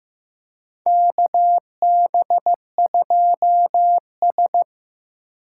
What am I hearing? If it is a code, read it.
Morse code, heard as a keyed tone, about 15 wpm, 700 Hz. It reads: KB2S
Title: K1USN SST QSO Practice - 15wpm